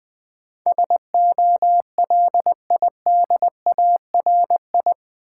Morse code, heard as a keyed tone, about 20 wpm, 700 Hz. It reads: SOLIDARI
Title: Build A Word - 4 Letters At A Time - 10-Letter Words - 20wpm